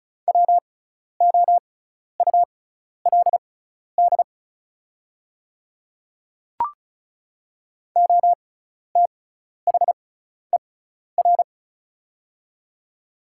Morse code, heard as a keyed tone, about 36 wpm, spaced as for 12 wpm, 700 Hz. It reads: WOULD OTHER
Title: 6x Character Spacing - Top 200 Words - 35wpm